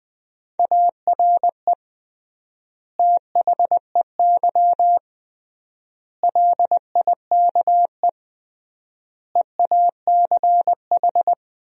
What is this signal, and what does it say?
Morse code, heard as a keyed tone, about 21 wpm, 700 Hz. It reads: ARE THEY LIKE EACH
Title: 3x Word-Spacing - Sentences from Top 200 Words - 20wpm